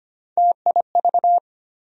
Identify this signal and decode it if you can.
Morse code, heard as a keyed tone, about 25 wpm, 700 Hz. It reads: TIV
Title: USA Call Sign Postfixes - 3 Letters 25wpm